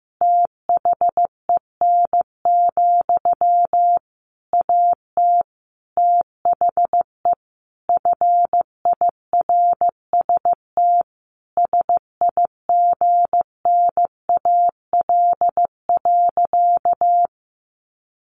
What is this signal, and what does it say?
Morse code, heard as a keyed tone, about 15 wpm, 700 Hz. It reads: THEN, AT THE FIRST SIGNAL.
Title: Sentences from Top 1400 Words 15wpm